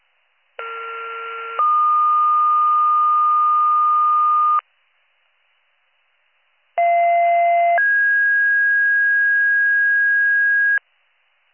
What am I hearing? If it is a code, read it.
Two-tone page, heard as a one-second tone, then a three-second tone, about 1370 then 1220 Hz, about 690 then 1650 Hz. It is heard from 0.6 to 10.8 s.